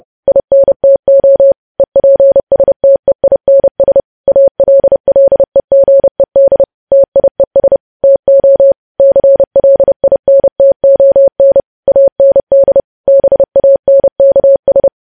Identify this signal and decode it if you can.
Morse code, heard as a keyed tone, about 30 wpm, 550 Hz. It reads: INTO EPSTEINS ALLEGED TIES TO CLINTON AND BANKS
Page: News Headlines in Morse Code at 30 WPM